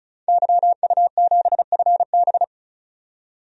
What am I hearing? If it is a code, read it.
Morse code, heard as a keyed tone, about 34 wpm, 700 Hz. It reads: YU7FB
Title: DXCC-100 Call Signs - 5 Characters - 35wpm